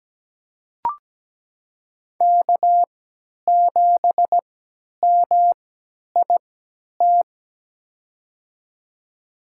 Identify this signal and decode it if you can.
Morse code, heard as a keyed tone, about 17 wpm, spaced as for 10 wpm, 700 Hz. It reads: K7MIT
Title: USA Call Signs - 5 Characters - 3x Character Spacing - 17wpm